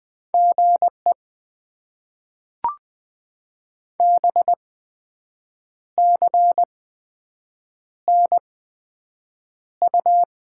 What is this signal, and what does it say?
Morse code, heard as a keyed tone, about 20 wpm, spaced as for 7 wpm, 700 Hz. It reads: GE BCNU
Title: Top 90 QSO Elements - 8x Character Spacing - 20wpm